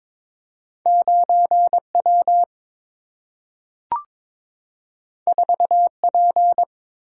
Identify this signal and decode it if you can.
Morse code, heard as a keyed tone, about 22 wpm, 700 Hz. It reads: 9W 4P
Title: Single-Letter Single-Number 22wpm